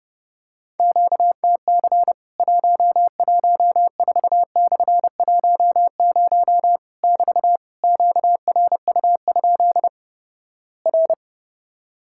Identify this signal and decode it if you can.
Morse code, heard as a keyed tone, about 30 wpm, 700 Hz. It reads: QTC 114/10 <BT> QRU? R
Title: Worked All Europe DX Contest - QTC Exchange - 30wpm